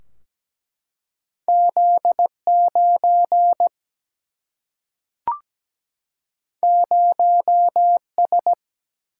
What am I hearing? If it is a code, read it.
Morse code, heard as a keyed tone, about 17 wpm, 700 Hz. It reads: Z 9 0 S
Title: Single-Letter Single-Number 17wpm